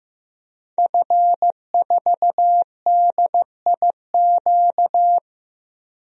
Morse code, heard as a keyed tone, about 15 wpm, 700 Hz. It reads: F4DIQ
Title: DXCC-100 Call Signs - 5 Characters - Repeated 3x - 15wpm